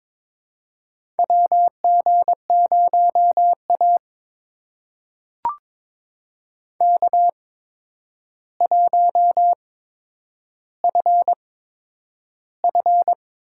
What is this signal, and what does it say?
Morse code, heard as a keyed tone, about 22 wpm, spaced as for 9 wpm, 700 Hz. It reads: WG0A K1FF
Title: USA Call Signs - 4 Characters - 8x Character Spacing - 22wpm